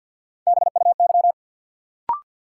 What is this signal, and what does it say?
Morse code, heard as a keyed tone, about 53 wpm, 700 Hz. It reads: BUY
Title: Sentences from Top 700 Words - New Word Drill - 50wpm